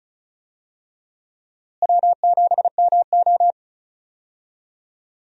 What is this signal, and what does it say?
Morse code, heard as a keyed tone, about 34 wpm, 700 Hz. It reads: W7MO
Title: USA Call Signs - 4 Characters - Repeated 3x - 35wpm